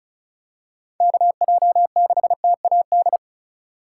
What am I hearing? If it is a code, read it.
Morse code, heard as a keyed tone, about 36 wpm, 700 Hz. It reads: KJ6TAD
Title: USA Call Signs - 6 Characters - 3x Character Spacing - 35wpm